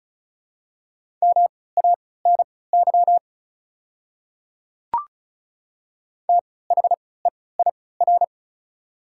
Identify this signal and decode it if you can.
Morse code, heard as a keyed tone, about 36 wpm, spaced as for 20 wpm, 690 Hz. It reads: MANY THEIR
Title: 3x Character Spacing - Top 100 Words - 35wpm